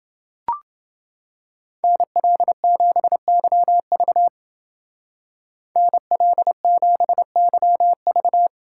Morse code, heard as a keyed tone, about 29 wpm, 700 Hz. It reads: NL7YV NL7YV